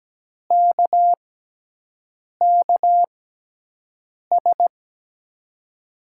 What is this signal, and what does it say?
Morse code, heard as a keyed tone, about 17 wpm, spaced as for 6 wpm, 700 Hz. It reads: KKS
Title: USA Call Signs - 6x Character Spacing - 17wpm